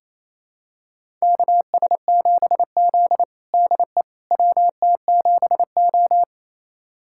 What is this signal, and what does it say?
Morse code, heard as a keyed tone, about 28 wpm, 700 Hz. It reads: KS7Z DE WT7O
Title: USA Callsign DE Callsign - 4 Characters - 28wpm